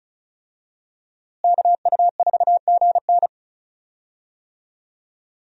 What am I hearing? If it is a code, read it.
Morse code, heard as a keyed tone, about 36 wpm, 700 Hz. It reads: KU4GN
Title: USA Call Signs - 5 Characters - Repeated 3x - 35wpm